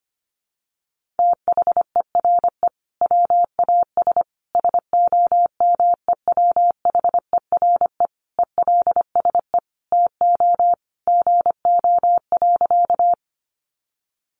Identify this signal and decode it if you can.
Morse code, heard as a keyed tone, about 25 wpm, 700 Hz. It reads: THERE WAS SOMEWHERE ELSE TO GO.
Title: Sentences from Top 1400 Words 25wpm